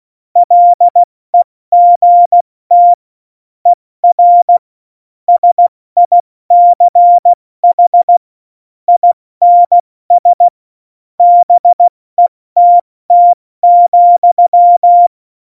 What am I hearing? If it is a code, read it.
Morse code, heard as a keyed tone, about 16 wpm, 700 Hz. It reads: LEGT ER SICH INS BETT,
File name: Morsekurs / Witze / Witz 27 Juni 17, Tempo 70 F80